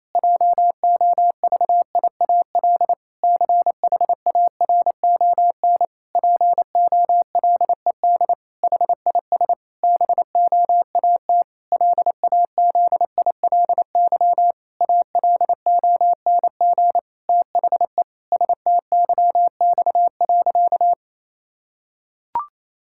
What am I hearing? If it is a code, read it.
Morse code, heard as a keyed tone, about 28 wpm, 700 Hz. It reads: JOVIAL CHARON POLED HIS BOAT LAZILY ALONG THE STYX.
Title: Pangram Challenge v2 - 28wpm